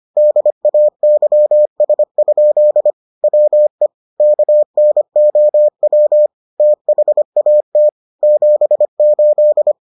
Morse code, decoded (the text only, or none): DAYS? WE KNOW THAT 78